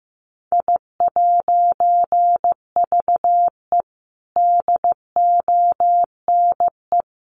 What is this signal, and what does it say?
Morse code, heard as a keyed tone, about 15 wpm, 700 Hz. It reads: I'VE DONE
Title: A Sound Of Thunder 15wpm